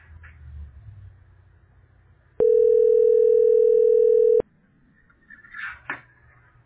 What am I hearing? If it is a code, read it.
Ringback tone (North America), 440 + 480 Hz.